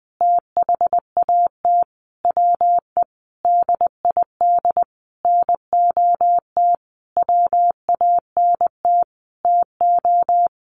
Morse code, heard as a keyed tone, about 20 wpm, 700 Hz. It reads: THAT WE DID NOT WANT TO
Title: Sentences from Top 1300 Words 20wpm